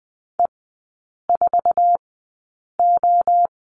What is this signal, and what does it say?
Morse code, heard as a keyed tone, about 20 wpm, spaced as for 8 wpm, 700 Hz. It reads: E4O
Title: 047 Lesson 9e - Number 4 - Three Characters 20wpm